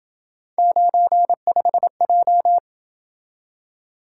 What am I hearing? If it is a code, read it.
Morse code, heard as a keyed tone, about 27 wpm, 700 Hz. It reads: 95J